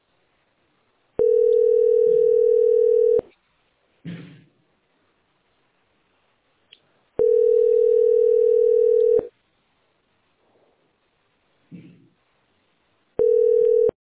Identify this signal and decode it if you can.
Ringback tone (North America), 440 + 480 Hz.